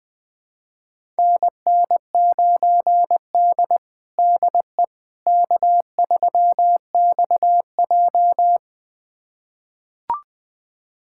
Morse code, heard as a keyed tone, about 20 wpm, 700 Hz. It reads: NN9D DE K3XJ
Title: USA Callsign DE Callsign - 4 Characters - 20wpm